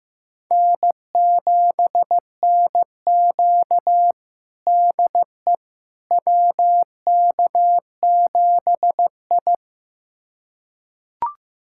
Morse code, heard as a keyed tone, about 15 wpm, 700 Hz. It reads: N7NQ DE WK7I
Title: USA Callsign DE Callsign - 4 Characters - 15wpm